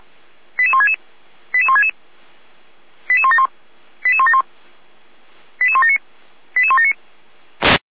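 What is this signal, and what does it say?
Five-tone selective call (ZVEI), heard as quick runs of five tones, tones of about 70 ms. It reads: 88166 88166 88171 88171 88179 88179
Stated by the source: ZVEI five-tone call 88521 received via SDR.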